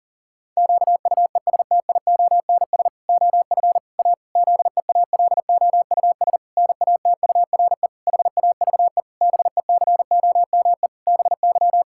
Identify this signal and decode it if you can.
Morse code, heard as a keyed tone, about 40 wpm, 700 Hz. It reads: QUESTIONS OF A ZEALOUS NATURE HAVE BECOME BY
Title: Pangram Challenge v2 - 40wpm